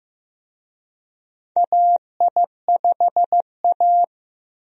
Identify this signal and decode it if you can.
Morse code, heard as a keyed tone, about 15 wpm, 700 Hz. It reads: AI5A
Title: USA Call Signs - 4 Characters - Repeated 2x - 15wpm